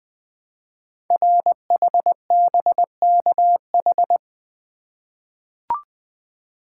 Morse code, heard as a keyed tone, about 20 wpm, 700 Hz. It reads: RHBKH
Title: Code Groups - 5 Letters - 20wpm